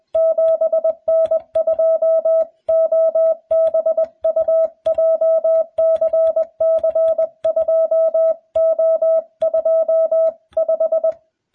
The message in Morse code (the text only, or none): 7N2OBU JCC2O25